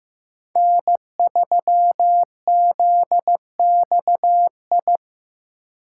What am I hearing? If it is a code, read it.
Morse code, heard as a keyed tone, about 15 wpm, 700 Hz. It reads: N3ZXI
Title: USA Call Signs - 5 Characters - 2x Character Spacing - 15wpm